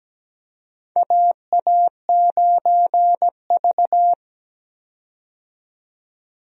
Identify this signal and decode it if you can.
Morse code, heard as a keyed tone, about 17 wpm, 700 Hz. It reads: AA9V
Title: USA Call Signs - 4 Characters - 17wpm